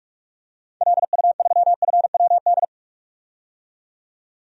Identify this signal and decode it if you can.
Morse code, heard as a keyed tone, about 48 wpm, 700 Hz. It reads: LU3FWD
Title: DXCC-100 Call Signs - 6 Characters - 45wpm